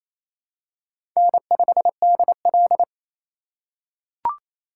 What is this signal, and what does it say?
Morse code, heard as a keyed tone, about 28 wpm, 700 Hz. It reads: N5DL
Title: USA Call Signs - 4 Characters - 3x Character Spacing - 28wpm